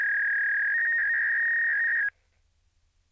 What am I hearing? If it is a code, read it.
Fax machine handshake.
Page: fax-dialling-fax-machine